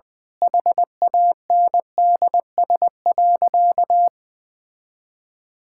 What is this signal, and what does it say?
Morse code, heard as a keyed tone, about 20 wpm, 700 Hz. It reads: HANDS.
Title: Sentences with 7-Word Phrases - Spoken Beginning - 20wpm